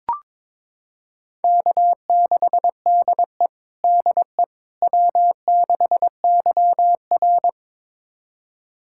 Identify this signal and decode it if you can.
Morse code, heard as a keyed tone, about 22 wpm, 700 Hz. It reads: K6DE DE W6YR
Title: USA Callsign DE Callsign - 4 Characters - 22wpm